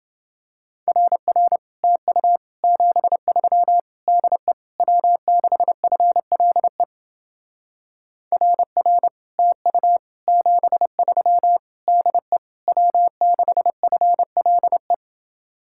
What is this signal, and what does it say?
Morse code, heard as a keyed tone, about 30 wpm, 700 Hz. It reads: RR TU 73 DE W6FLE RR TU 73 DE W6FLE